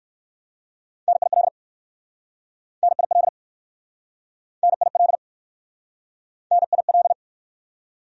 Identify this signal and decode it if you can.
Morse code, heard as a keyed tone, about 60 wpm, 700 Hz. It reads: NIL NIL NIL NIL